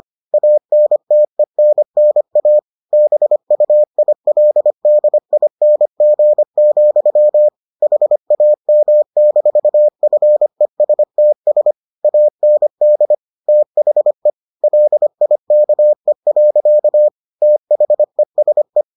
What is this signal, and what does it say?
Morse code, heard as a keyed tone, about 25 wpm, 600 Hz. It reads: ANTENNA BUILDING, HAM-FESTS AND THE LIKE. THESE